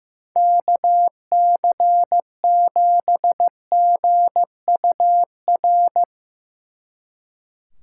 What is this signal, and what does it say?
Morse code, heard as a keyed tone, about 15 wpm, 700 Hz. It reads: KC7GUR
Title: Speed Racing - USA 2x3 Call Signs 15wpm